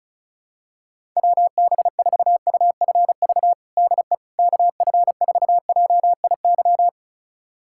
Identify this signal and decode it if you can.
Morse code, heard as a keyed tone, about 35 wpm, 700 Hz. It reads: WB4UFV DE KF4JIY
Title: USA Callsign DE Callsign - 6 Characters - 35wpm